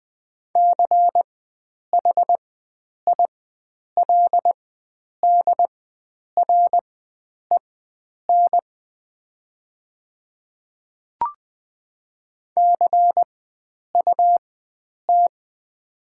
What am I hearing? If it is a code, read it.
Morse code, heard as a keyed tone, about 20 wpm, spaced as for 9 wpm, 700 Hz. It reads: CHILDREN CUT
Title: 4x Character Spacing - Top 300 Words - 20wpm